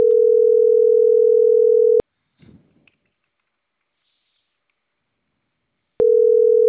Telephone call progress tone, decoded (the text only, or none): ringback tone